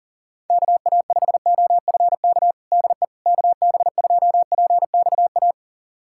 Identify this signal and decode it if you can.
Morse code, heard as a keyed tone, about 40 wpm, 700 Hz. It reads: KA5OFK DE KB2PXA
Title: USA Callsign DE Callsign - 6 Characters - 40wpm